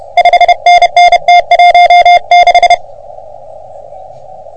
Morse code, heard as a keyed tone, about 30 wpm, 720 Hz. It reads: 5NN T16
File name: OM3PA PILUP CUC 388